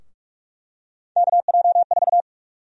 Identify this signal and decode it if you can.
Morse code, heard as a keyed tone, about 46 wpm, 700 Hz. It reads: KJ4
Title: USA Call Sign Prefixes - 45wpm